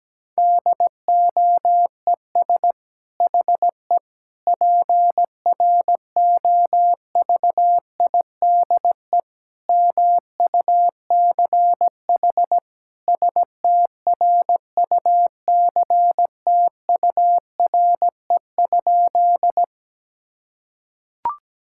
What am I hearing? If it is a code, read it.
Morse code, heard as a keyed tone, about 17 wpm, 700 Hz. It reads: DOES HE PROVIDE MUCH STRUCTURE?
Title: Sentences from Top 1000 Words - Review Part 6 - 17wpm